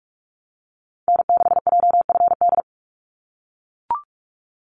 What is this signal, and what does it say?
Morse code, heard as a keyed tone, about 44 wpm, 700 Hz. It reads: N6JFD
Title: NA SOTA Activator Callsigns 45wpm